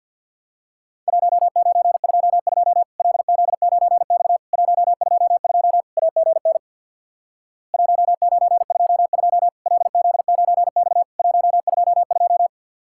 Morse code, heard as a keyed tone, about 50 wpm, 700 Hz. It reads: 1922 LZ9X 122 AGN 1922 LZ9X 122
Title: Worked All Europe DX Contest - QTC Exchange with Repeats - 50wpm